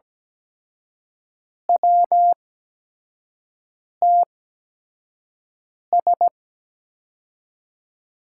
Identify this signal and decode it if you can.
Morse code, heard as a keyed tone, about 17 wpm, spaced as for 5 wpm, 700 Hz. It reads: WTS
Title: USA Call Signs - 8x Character Spacing - 17wpm